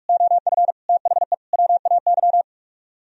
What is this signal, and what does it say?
Morse code, heard as a keyed tone, about 44 wpm, 700 Hz. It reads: OF THE WAY